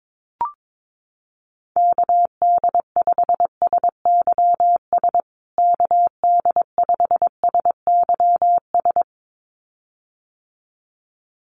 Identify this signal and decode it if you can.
Morse code, heard as a keyed tone, about 22 wpm, 700 Hz. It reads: KD5SYS KD5SYS
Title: USA Call Signs Repeated 2x 22wpm